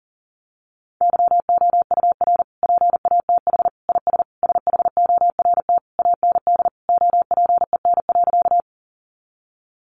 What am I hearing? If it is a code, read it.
Morse code, heard as a keyed tone, about 40 wpm, 700 Hz. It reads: YOUR PATH IS SHORT AND OPEN.
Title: Sentences from Top 1400 Words 40wpm